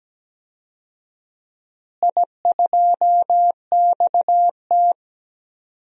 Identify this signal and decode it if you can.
Morse code, heard as a keyed tone, about 17 wpm, 700 Hz. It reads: I2XT
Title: DXCC-100 Call Signs - 4 Characters - Repeated 2x - 17wpm